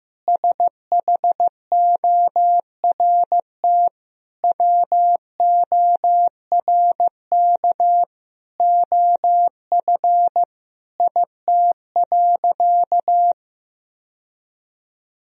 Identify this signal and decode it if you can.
Morse code, heard as a keyed tone, about 16 wpm, 700 Hz. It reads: SHORT WORK OF IT.
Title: Sentences from Top 1000 Words - Review Part 2 - 15wpm